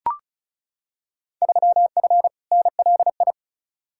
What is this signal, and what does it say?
Morse code, heard as a keyed tone, about 34 wpm, 700 Hz. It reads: 3F NLI
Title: Field Day Exchanges - 35wpm